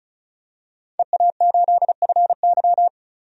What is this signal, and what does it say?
Morse code, heard as a keyed tone, about 36 wpm, 700 Hz. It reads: EA8FY
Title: DXCC-100 Call Signs - 5 Characters - 35wpm